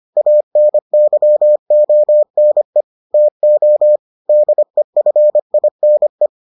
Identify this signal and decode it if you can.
Morse code, heard as a keyed tone, about 25 wpm, 600 Hz. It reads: ANYONE TO DEFINE